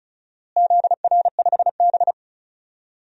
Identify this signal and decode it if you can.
Morse code, heard as a keyed tone, about 36 wpm, 700 Hz. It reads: ZR5B
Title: DXCC-100 Call Signs - 4 Characters - Repeated 3x - 35wpm